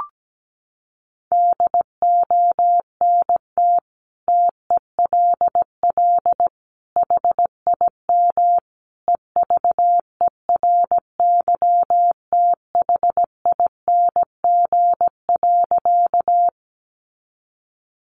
Morse code, heard as a keyed tone, about 17 wpm, 700 Hz. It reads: DONT TELL HIM EVERYTHING.
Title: Sentences from Top 700 Words 17wpm